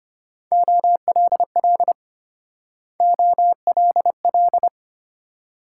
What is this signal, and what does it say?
Morse code, heard as a keyed tone, about 32 wpm, 700 Hz. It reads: OLL OLL